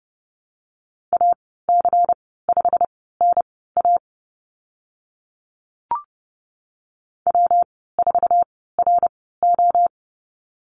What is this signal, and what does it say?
Morse code, heard as a keyed tone, about 30 wpm, spaced as for 17 wpm, 700 Hz. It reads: AC5NA W4RO
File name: Lesson 19g - Letter w - Call Signs 30wpm